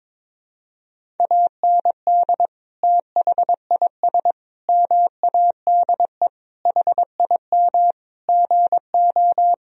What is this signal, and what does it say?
Morse code, heard as a keyed tone, about 21 wpm, 700 Hz. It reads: AND THIS MADE HIM GO
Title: Sentences from Top 1000 Words - Review Part 1 - 22wpm